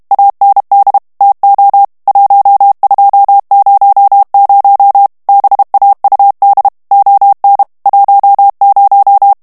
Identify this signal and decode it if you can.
Morse code, heard as a keyed tone, about 32 wpm, 800 Hz. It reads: AND TO 1200 BAUD ON 10
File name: ANGLAIS texte clair 3 vitesse 30 mots minute